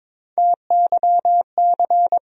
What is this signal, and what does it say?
Morse code, heard as a keyed tone, about 21 wpm, 700 Hz. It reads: TYC
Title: USA Call Sign Postfixes - 3 Letters 22wpm